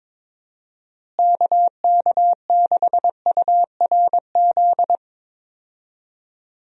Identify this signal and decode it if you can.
Morse code, heard as a keyed tone, about 21 wpm, 700 Hz. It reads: KK6URZ
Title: USA Call Signs - 6 Characters - Repeated 3x - 22wpm